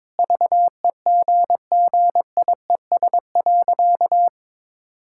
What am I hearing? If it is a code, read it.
Morse code, heard as a keyed tone, about 21 wpm, 700 Hz. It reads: VEGGIES.
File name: practice / 22wpm / Easy Sentences with Hints - 22wpm